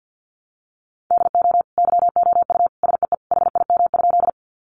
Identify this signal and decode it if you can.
Morse code, heard as a keyed tone, about 50 wpm, 700 Hz. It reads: DO YOU SEE HIM?
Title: Sentences from Top 100 Words 50wpm